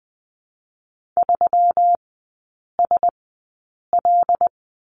Morse code, heard as a keyed tone, about 20 wpm, spaced as for 8 wpm, 700 Hz. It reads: 3SL
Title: 218 Lesson 35e - Number 0 - Three Characters 20wpm